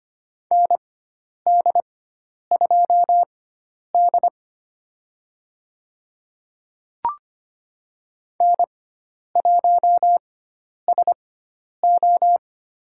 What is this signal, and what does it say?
Morse code, heard as a keyed tone, about 25 wpm, spaced as for 10 wpm, 700 Hz. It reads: ND2D N1SO
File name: Lesson 16g - Letter c - Call Signs 25wpm